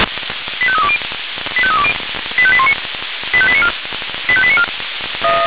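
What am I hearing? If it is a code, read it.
Five-tone selective call (ZVEI), heard as quick runs of five tones, tones of about 70 ms. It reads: 85320 85320 85819 85894 85894
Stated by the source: ZVEI five-tone selective call received on SDR.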